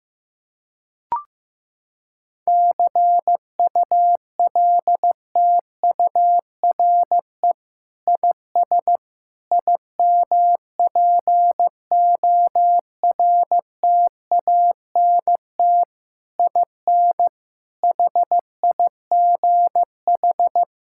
Morse code, heard as a keyed tone, about 15 wpm, 700 Hz. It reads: CULTURE IS IMPORTANT IN HIGH
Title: Sentences from Top 1000 Words - Review Part 6 - 15wpm